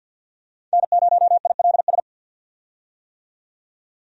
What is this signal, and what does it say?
Morse code, heard as a keyed tone, about 52 wpm, 700 Hz. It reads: N0ILS
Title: USA Call Signs - 5 Characters - 50wpm